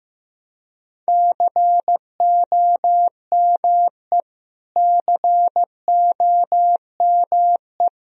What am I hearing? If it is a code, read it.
Morse code, heard as a keyed tone, about 15 wpm, 700 Hz. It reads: COME COME
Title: Top 100 Words Repeated 15wpm